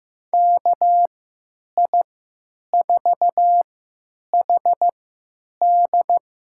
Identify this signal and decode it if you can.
Morse code, heard as a keyed tone, about 15 wpm, spaced as for 9 wpm, 700 Hz. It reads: KI4HD
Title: USA Call Signs - 6 Characters - 3x Character Spacing - 15wpm